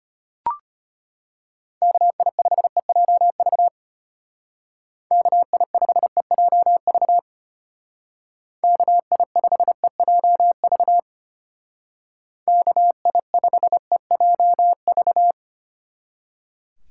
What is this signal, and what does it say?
Morse code, heard as a keyed tone, about 31 wpm, 700 Hz. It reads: KI5EJV KI5EJV KI5EJV KI5EJV
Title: Speed Racing - USA Call Signs 25wpm